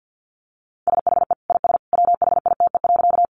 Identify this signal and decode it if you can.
Morse code, heard as a keyed tone, about 50 wpm, 700 Hz. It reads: SHE IS WHITE.
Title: Easy Phrases 50wpm